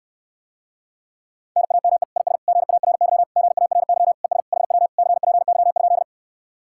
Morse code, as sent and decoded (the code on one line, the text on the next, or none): -. .- -- . / .. ... / --. .- .-. -.-- / --. .- .-. -.-- / . ... / .... .-- / -.-. .--. -.-- ..--..
NAME IS GARY GARY ES HW CPY?